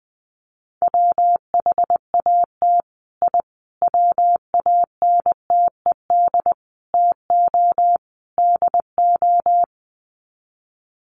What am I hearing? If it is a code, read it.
Morse code, heard as a keyed tone, about 20 wpm, 700 Hz. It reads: WHAT I WANTED TO DO
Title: Sets of 5 Words Encore2 20wpm